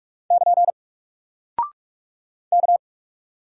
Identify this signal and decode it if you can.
Morse code, heard as a keyed tone, about 44 wpm, 700 Hz. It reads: <KN> K
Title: Single Character (Letter:Number:Punctuation:Prosign) Alt 45wpm